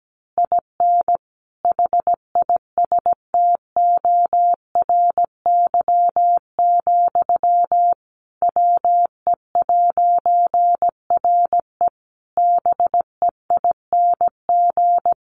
Morse code, heard as a keyed tone, about 17 wpm, 700 Hz. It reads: IN HISTORY, WE'RE BEING
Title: A Sound Of Thunder 17wpm